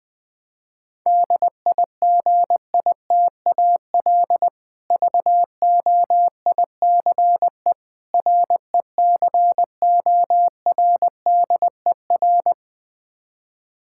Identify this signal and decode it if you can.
Morse code, heard as a keyed tone, about 20 wpm, 700 Hz. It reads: DIGITAL VOICE RECORDER
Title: Difficult Words - Speed-Racing with Hints - 20wpm